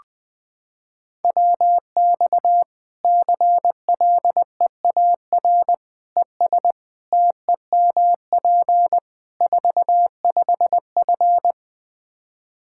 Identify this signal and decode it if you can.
Morse code, heard as a keyed tone, about 21 wpm, 700 Hz. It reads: WX CLEAR ES TEMP 45F
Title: Practice QSO - Exchange 2 - Info - 20wpm